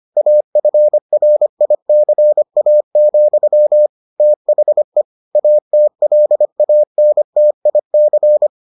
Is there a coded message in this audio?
Yes: Morse code from 0.2 to 8.5 s, heard as a keyed tone, about 25 wpm, 600 Hz. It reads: AFRICA, THE ATLANTIC